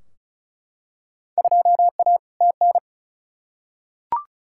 Morse code, heard as a keyed tone, about 36 wpm, 700 Hz. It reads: 2A TN
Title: Field Day Exchanges - 35wpm